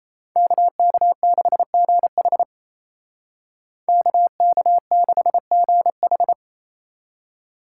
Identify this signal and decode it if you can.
Morse code, heard as a keyed tone, about 34 wpm, 700 Hz. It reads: KK6GH KK6GH
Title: Speed Racing - USA 2x2 Call Signs 28wpm